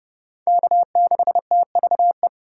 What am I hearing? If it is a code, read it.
Morse code, heard as a keyed tone, about 32 wpm, 700 Hz. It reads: K6TVE
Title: USA Call Signs - 5 Characters - 3x Character Spacing - 30wpm